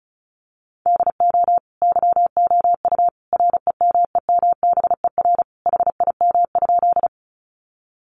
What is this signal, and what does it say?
Morse code, heard as a keyed tone, about 34 wpm, 700 Hz. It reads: DO YOU REMEMBER HIM?
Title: Sentences from Top 400 Words 35wpm